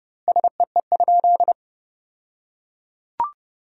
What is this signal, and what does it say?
Morse code, heard as a keyed tone, about 30 wpm, 700 Hz. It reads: SEE?
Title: 2x Word-Spacing - Sentences from Top 100 Words - 30wpm